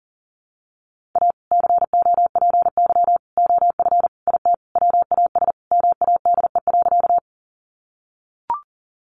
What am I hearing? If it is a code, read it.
Morse code, heard as a keyed tone, about 40 wpm, 700 Hz. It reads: A COPY OF IT WAS MADE.
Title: Sentences from Top 1400 Words 40wpm